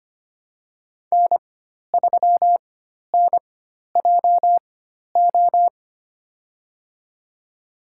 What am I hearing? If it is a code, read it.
Morse code, heard as a keyed tone, about 25 wpm, spaced as for 12 wpm, 700 Hz. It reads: N3NJO
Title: USA Call Signs - 5 Characters - 4x Character Spacing - 25wpm